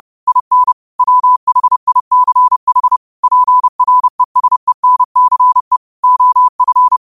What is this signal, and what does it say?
Morse code, heard as a keyed tone, about 30 wpm, 1000 Hz. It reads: IN WHICH PRESENCE OF
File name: qotdfiles / 30wpm